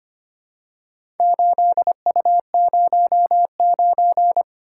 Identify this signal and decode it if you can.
Morse code, heard as a keyed tone, about 25 wpm, 700 Hz. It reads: 8U09
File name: Four Letter-Number Combinations - 25wpm